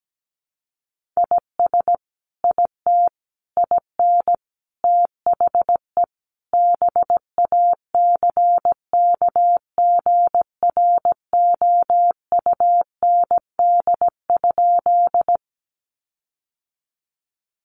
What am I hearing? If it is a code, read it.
Morse code, heard as a keyed tone, about 17 wpm, 700 Hz. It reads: IS IT IN THE BACKGROUND?
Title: Sentences from Top 1300 Words 17wpm